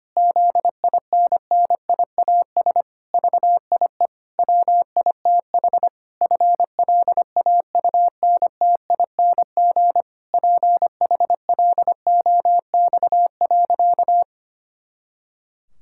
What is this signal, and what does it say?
Morse code, heard as a keyed tone, about 25 wpm, 700 Hz. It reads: ZINNIAS VIE WITH FLAUNTING PHLOX.
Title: Pangram Challenge v2 - 25wpm